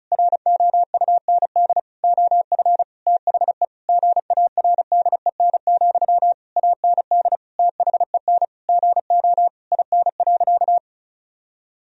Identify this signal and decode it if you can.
Morse code, heard as a keyed tone, about 36 wpm, 700 Hz. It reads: ROUND OF THE GARDEN, AND THEN GO IN.
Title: Sentences from Top 1000 Words - Review Part 4 - 35wpm